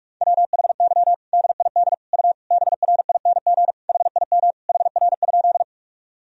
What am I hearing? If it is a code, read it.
Morse code, heard as a keyed tone, about 46 wpm, 700 Hz. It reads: WHY DID U BRING HIM HR?
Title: Sentences from Top 500 Words and Abbreviations 45wpm